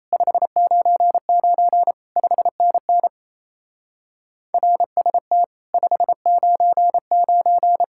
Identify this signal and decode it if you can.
Morse code, heard as a keyed tone, about 37 wpm, 700 Hz. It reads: 599 5NN RST 599